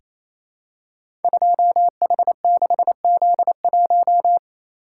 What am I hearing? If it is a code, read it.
Morse code, heard as a keyed tone, about 28 wpm, 700 Hz. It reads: 2H6Z1
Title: Code Groups - 5 Letter:Numbers - 28wpm